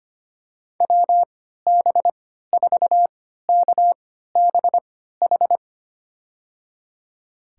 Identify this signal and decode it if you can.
Morse code, heard as a keyed tone, about 25 wpm, spaced as for 14 wpm, 700 Hz. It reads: WB4KBH
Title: USA Call Signs - 6 Characters - 3x Character Spacing - 25wpm